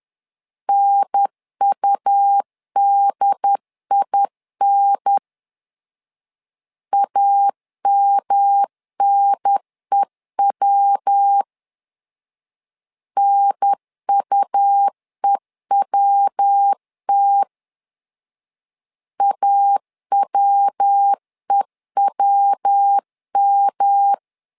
Morse code, heard as a keyed tone, about 10 wpm, 800 Hz. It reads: NUDIN AMNEW NUEWT AWEWM
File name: LEZIONI / 3 MWRO / 3 MW 10wpm 1959